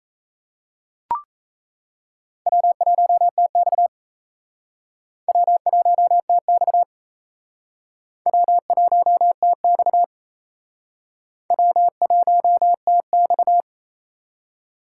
Morse code, heard as a keyed tone, about 36 wpm, 700 Hz. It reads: W1TX W1TX W1TX W1TX